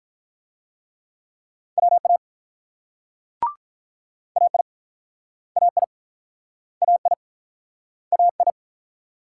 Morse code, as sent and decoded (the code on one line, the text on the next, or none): .-- .- / .- .. / .- .. / .- .. / .- ..
WA AI AI AI AI